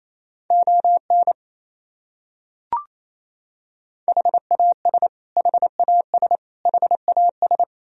Morse code, heard as a keyed tone, about 28 wpm, 700 Hz. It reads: ON HAS HAS HAS
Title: Top 100 Words Repeated 28wpm